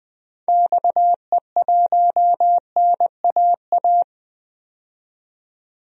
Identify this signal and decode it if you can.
Morse code, heard as a keyed tone, about 20 wpm, 700 Hz. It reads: XE1NAA
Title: DXCC-100 Call Signs - 6 Characters - 20wpm